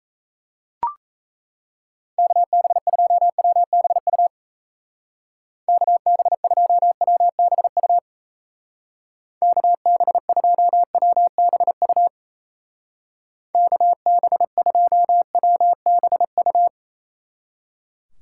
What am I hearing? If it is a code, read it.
Morse code, heard as a keyed tone, about 34 wpm, 700 Hz. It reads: KB2WBU KB2WBU KB2WBU KB2WBU
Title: Speed Racing - USA Call Signs 28wpm